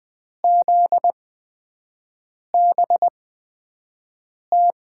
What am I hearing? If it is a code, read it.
Morse code, heard as a keyed tone, about 20 wpm, spaced as for 5 wpm, 700 Hz. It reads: ZBT
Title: USA Call Signs - 5 Characters - 8x Character Spacing - 20wpm